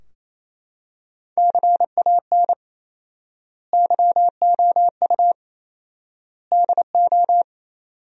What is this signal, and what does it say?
Morse code, heard as a keyed tone, about 28 wpm, 700 Hz. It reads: CAN YOU DO
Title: 4x Word-Spacing - Sentences from Top 100 Words - 28wpm